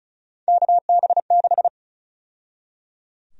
Morse code, heard as a keyed tone, about 34 wpm, 700 Hz. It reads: KB6
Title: USA Call Sign Prefixes - 35wpm